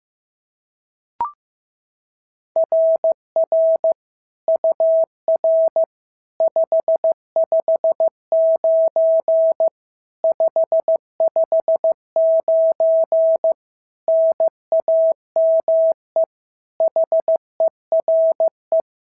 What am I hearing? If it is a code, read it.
Morse code, heard as a keyed tone, about 14 wpm, 650 Hz. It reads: RR UR 559 559 NAME HERE
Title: SOTA QSO Practice 15wpm